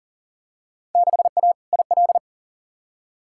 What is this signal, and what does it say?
Morse code, heard as a keyed tone, about 40 wpm, 700 Hz. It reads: 6A IL
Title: Field Day Exchanges - 40wpm